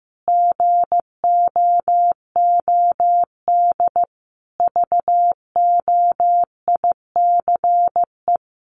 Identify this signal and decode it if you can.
Morse code, heard as a keyed tone, about 15 wpm, 700 Hz. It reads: GOOD VOICE
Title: Sentences from Top 500 Words 15wpm